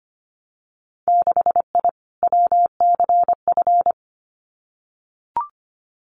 Morse code, heard as a keyed tone, about 25 wpm, 690 Hz. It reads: 6I WCF
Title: Winter Field Day Exchanges Encore 25wpm